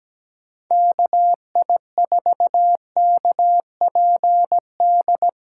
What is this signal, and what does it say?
Morse code, heard as a keyed tone, about 17 wpm, 700 Hz. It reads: KI4KPD
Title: USA Call Signs - 6 Characters - 8x Character Spacing - 17wpm